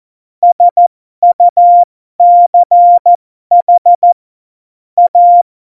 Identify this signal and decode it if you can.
Morse code, heard as a keyed tone, about 14 wpm, spaced as for 12 wpm, 700 Hz. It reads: SUCH A
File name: Morsekurs / Witze